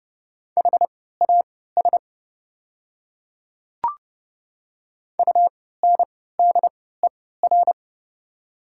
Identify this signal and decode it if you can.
Morse code, heard as a keyed tone, about 30 wpm, spaced as for 17 wpm, 700 Hz. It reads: HAS UNDER
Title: 3x Character Spacing - Top 200 Words - 30wpm